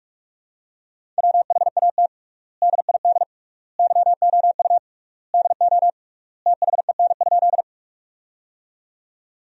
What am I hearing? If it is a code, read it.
Morse code, heard as a keyed tone, about 45 wpm, 700 Hz. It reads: WHAT DID YOU DO THEN?
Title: 3x Word-Spacing - Sentences from Top 100 Words - 45wpm